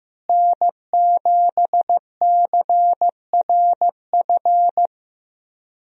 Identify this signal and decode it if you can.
Morse code, heard as a keyed tone, about 15 wpm, 700 Hz. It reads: N7CRF
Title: USA Call Signs - 5 Characters - Repeated 3x - 15wpm